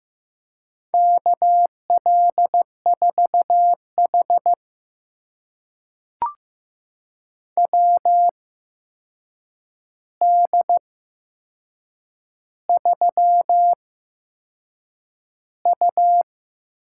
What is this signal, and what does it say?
Morse code, heard as a keyed tone, about 15 wpm, spaced as for 6 wpm, 700 Hz. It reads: KL4H WD3U
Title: USA Call Signs - 4 Characters - 8x Character Spacing - 15wpm